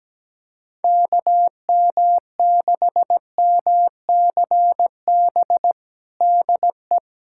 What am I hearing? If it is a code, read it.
Morse code, heard as a keyed tone, about 17 wpm, 700 Hz. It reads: KM6MCB DE
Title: USA Callsign DE Callsign 17wpm